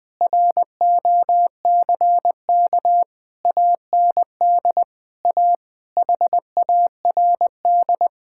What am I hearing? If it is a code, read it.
Morse code, heard as a keyed tone, about 20 wpm, 700 Hz. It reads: ROCK AND A HARD
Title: Sets of 8 Words - 20wpm